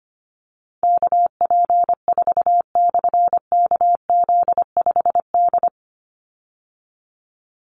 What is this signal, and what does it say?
Morse code, heard as a keyed tone, about 25 wpm, 700 Hz. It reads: KP4/KZ5D
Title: Call Signs Repeated 2x - 25wpm